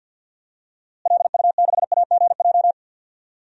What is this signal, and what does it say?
Morse code, heard as a keyed tone, about 45 wpm, 700 Hz. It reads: LU6AGJ
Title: DXCC-100 Call Signs - 6 Characters - Repeated 3x - 50wpm